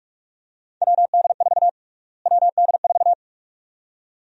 Morse code, heard as a keyed tone, about 46 wpm, 690 Hz. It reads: WD4 WD4
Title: USA Call Sign Prefixes - Repeated 2x - 45wpm